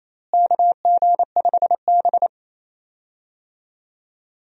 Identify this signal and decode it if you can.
Morse code, heard as a keyed tone, about 28 wpm, 700 Hz. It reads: KG5B
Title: USA Call Signs - 4 Characters - Repeated 2x - 28wpm